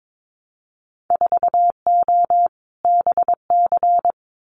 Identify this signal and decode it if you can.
Morse code, heard as a keyed tone, about 22 wpm, 700 Hz. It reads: 4O BC
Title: Winter Field Day Exchanges Encore 22wpm